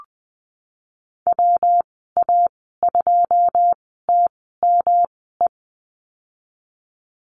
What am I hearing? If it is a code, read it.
Morse code, heard as a keyed tone, about 19 wpm, spaced as for 14 wpm, 700 Hz. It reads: WA2TME
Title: 154 Lesson 25g - Letter p - Call Signs 20wpm